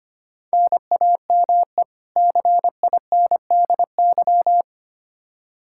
Morse code, heard as a keyed tone, about 25 wpm, 700 Hz. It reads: NAME CINDY
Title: K1USN SST QSO Practice 25wpm